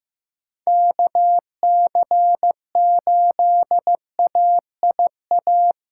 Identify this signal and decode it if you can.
Morse code, heard as a keyed tone, about 16 wpm, 700 Hz. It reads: KC8AIA
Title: SOTA QSO Practice 15wpm